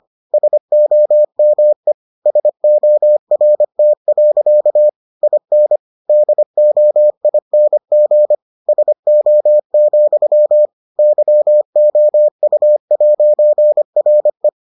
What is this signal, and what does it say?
Morse code, heard as a keyed tone, about 25 wpm, 600 Hz. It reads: SOME SORT. IN DOING SO, YOU'RE